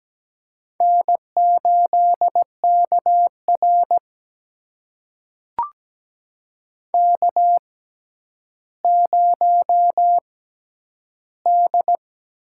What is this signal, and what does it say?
Morse code, heard as a keyed tone, about 17 wpm, spaced as for 10 wpm, 700 Hz. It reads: N8KR K0D
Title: USA Call Signs - 4 Characters - 6x Character Spacing - 17wpm